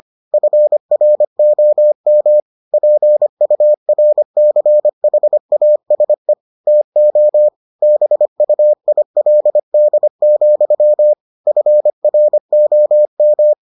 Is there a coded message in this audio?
Yes: Morse code from 0.3 to 13.5 s, heard as a keyed tone, about 25 wpm, 600 Hz. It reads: FROM PURCHASE TO BUILD, FROM